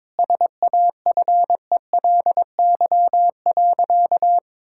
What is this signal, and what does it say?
Morse code, heard as a keyed tone, about 21 wpm, 700 Hz. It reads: SAFELY.
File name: practice / 22wpm / Easy Sentences with Hints - 22wpm